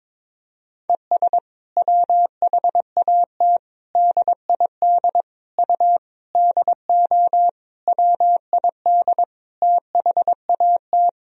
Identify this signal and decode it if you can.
Morse code, heard as a keyed tone, about 22 wpm, 700 Hz. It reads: ES WHAT DID U DO WID THAT
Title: Sentences from Top 300 Words and Abbreviations 22wpm